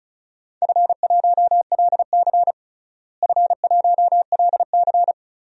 Morse code, heard as a keyed tone, about 36 wpm, 690 Hz. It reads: F1LC F1LC
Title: DXCC-100 Call Signs - 4 Characters - Repeated 3x - 35wpm